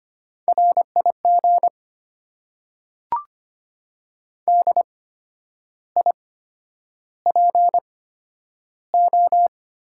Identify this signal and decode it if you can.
Morse code, heard as a keyed tone, about 25 wpm, spaced as for 10 wpm, 700 Hz. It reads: RIG DIPO
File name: Top 90 QSO Elements - 8x Character Spacing - 25wpm